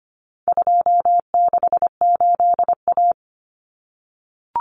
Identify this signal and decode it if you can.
Morse code, heard as a keyed tone, about 25 wpm, 700 Hz. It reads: 268A
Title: Common Cut Numbers - 4-Digit 25wpm